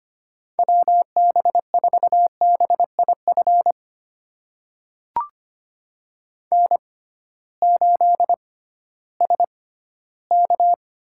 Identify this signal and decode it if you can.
Morse code, heard as a keyed tone, about 25 wpm, spaced as for 15 wpm, 700 Hz. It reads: WB4BIF N8SK
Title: USA Call Signs - 6x Character Spacing - 25wpm